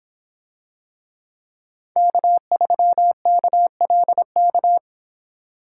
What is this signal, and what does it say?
Morse code, heard as a keyed tone, about 25 wpm, 700 Hz. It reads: K3KLK